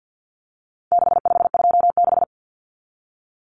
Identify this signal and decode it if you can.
Morse code, heard as a keyed tone, about 50 wpm, 700 Hz. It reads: 6526
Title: Numbers - 4-Digit 50wpm